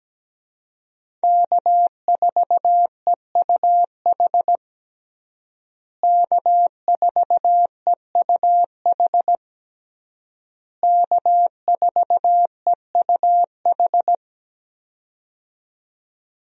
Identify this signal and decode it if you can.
Morse code, heard as a keyed tone, about 17 wpm, 700 Hz. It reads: K4EUH K4EUH K4EUH
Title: USA Call Signs - 5 Characters - Repeated 3x - 17wpm